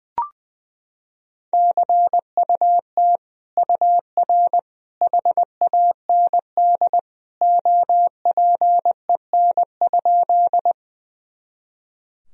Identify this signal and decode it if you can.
Morse code, heard as a keyed tone, about 20 wpm, 700 Hz. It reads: CUT UR HAND OPEN?
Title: Sentences from Top 300 Words and Abbreviations 20wpm